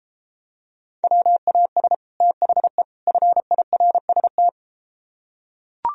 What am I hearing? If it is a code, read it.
Morse code, heard as a keyed tone, about 32 wpm, 700 Hz. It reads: WAS THE FIRST